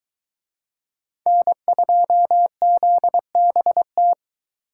Morse code, heard as a keyed tone, about 23 wpm, 700 Hz. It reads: N2ZBT